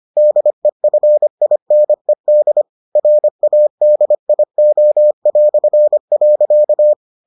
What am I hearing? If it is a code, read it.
Morse code, heard as a keyed tone, about 25 wpm, 600 Hz. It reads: DEFINED RADIO".